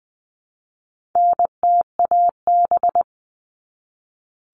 Morse code, heard as a keyed tone, about 20 wpm, 700 Hz. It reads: NTAB
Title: All Cut Numbers - 4-Digit 20wpm